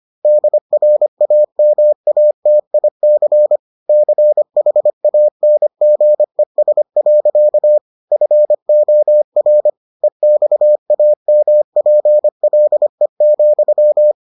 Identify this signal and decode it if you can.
Morse code, heard as a keyed tone, about 25 wpm, 600 Hz. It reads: DRAMATIC CHANGES. FOR EXAMPLE,